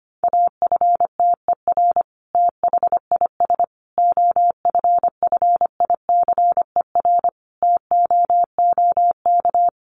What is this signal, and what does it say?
Morse code, heard as a keyed tone, about 25 wpm, 700 Hz. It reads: AFTER THIS OFFICER TOOK
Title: Sentences from Top 1200 Words 25wpm